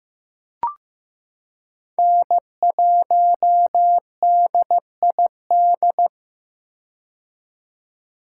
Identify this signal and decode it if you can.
Morse code, heard as a keyed tone, about 15 wpm, 700 Hz. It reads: N1DID
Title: K1USN SST QSO Practice - 15wpm